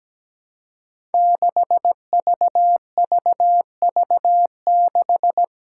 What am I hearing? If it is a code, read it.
Morse code, heard as a keyed tone, about 17 wpm, 700 Hz. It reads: 6VVV6
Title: Five Challenging Letter-Number Combinations 17wpm